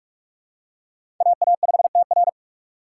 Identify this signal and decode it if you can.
Morse code, heard as a keyed tone, about 44 wpm, 700 Hz. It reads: AA5TR
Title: USA Call Signs - 6x Character Spacing - 45wpm